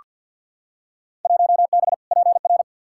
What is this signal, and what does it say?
Morse code, heard as a keyed tone, about 46 wpm, 700 Hz. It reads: 1B PR
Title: Field Day Exchanges Encore - 50wpm